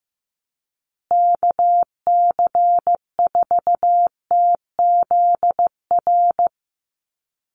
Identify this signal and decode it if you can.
Morse code, heard as a keyed tone, about 15 wpm, 700 Hz. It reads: KC4TZR
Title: USA Call Signs Repeated 3x 15wpm